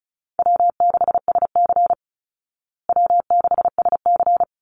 Morse code, heard as a keyed tone, about 34 wpm, 700 Hz. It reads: W6SC W6SC
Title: Call Signs Repeated 3x - 35wpm